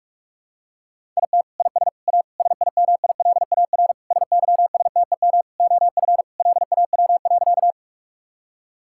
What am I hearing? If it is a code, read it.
Morse code, heard as a keyed tone, about 46 wpm, 700 Hz. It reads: IT IS A SIMILAR SYSTEM OF LAW.
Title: Sentences from Top 1000 Words - Review Part 5 - 45wpm